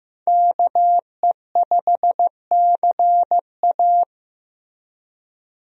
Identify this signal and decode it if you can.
Morse code, heard as a keyed tone, about 15 wpm, 700 Hz. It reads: KE5CA
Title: USA Call Signs - 5 Characters - Repeated 2x - 15wpm